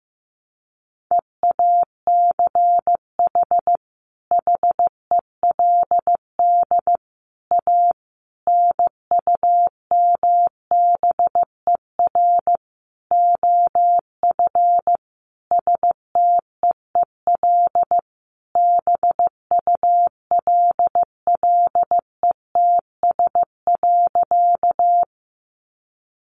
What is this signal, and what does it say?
Morse code, heard as a keyed tone, about 14 wpm, 700 Hz. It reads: EACH HELD A NUMBER OF STEEL BULLETS.
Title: A Sound Of Thunder 15wpm